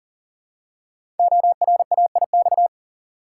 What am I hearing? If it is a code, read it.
Morse code, heard as a keyed tone, about 40 wpm, 690 Hz. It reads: ORAIX
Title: Code Groups - 5 Letters - 40wpm